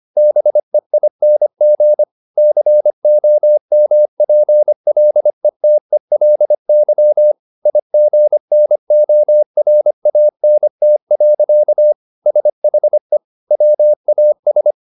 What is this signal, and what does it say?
Morse code, heard as a keyed tone, about 25 wpm, 600 Hz. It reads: BEING COMPLETELY IGNORANT. SHE WAS